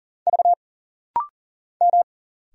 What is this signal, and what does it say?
Morse code, heard as a keyed tone, about 40 wpm, 700 Hz. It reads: V M
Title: Single Letters - Rapid-Fire 40wpm